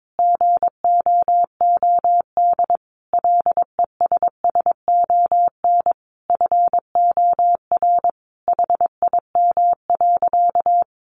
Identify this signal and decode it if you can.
Morse code, heard as a keyed tone, about 22 wpm, 700 Hz. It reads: GOOD LESSON FOR HIM.
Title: Sentences from Top 1200 Words 22wpm